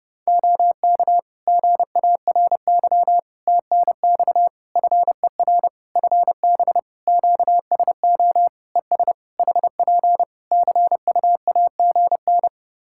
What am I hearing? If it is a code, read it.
Morse code, heard as a keyed tone, about 30 wpm, 700 Hz. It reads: OK GARY TNX FER FB QSO ES HP CUAGN
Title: Practice QSO - Exchange 4 - Ending - 30wpm